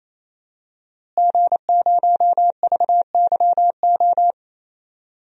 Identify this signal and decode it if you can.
Morse code, heard as a keyed tone, about 27 wpm, 700 Hz. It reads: G0VYO
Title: DXCC-100 Call Signs - 5 Characters - Repeated 3x - 28wpm